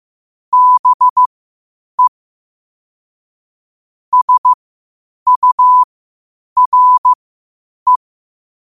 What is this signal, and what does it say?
Morse code, heard as a keyed tone, about 15 wpm, spaced as for 8 wpm, 1000 Hz. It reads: BE SURE
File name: qotdfiles / 8wpm